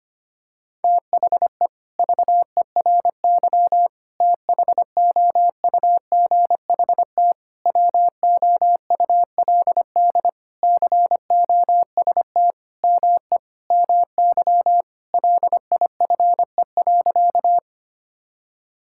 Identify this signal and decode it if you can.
Morse code, heard as a keyed tone, about 25 wpm, 700 Hz. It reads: THE VERY THOUGHT WOULD COST ME MY LIFE.
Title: Sentences from Top 1000 Words - Review Part 5 - 25wpm